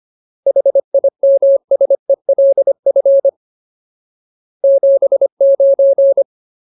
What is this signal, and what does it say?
Morse code, heard as a keyed tone, about 25 wpm, 550 Hz. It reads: HIMSELF 79
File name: Ultimate Word List / 25WPM Ultimate